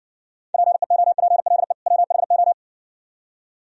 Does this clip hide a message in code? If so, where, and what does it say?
Yes: Morse code from 0.5 to 2.5 s, heard as a keyed tone, about 58 wpm, 700 Hz. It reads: PEOPLE WHO